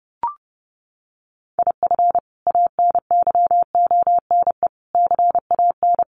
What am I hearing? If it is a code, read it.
Morse code, heard as a keyed tone, about 30 wpm, 700 Hz. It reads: IF ANYONE CAN
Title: Sentences from Top 1500 Words 30wpm